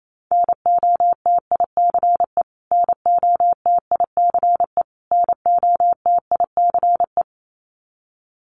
Morse code, heard as a keyed tone, about 28 wpm, 700 Hz. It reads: NOTICE NOTICE NOTICE
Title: Top 500 Words Repeated 28wpm